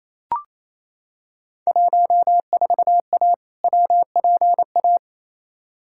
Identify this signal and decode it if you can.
Morse code, heard as a keyed tone, about 27 wpm, 700 Hz. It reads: 14A WPA
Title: Field Day Exchanges Encore - 28wpm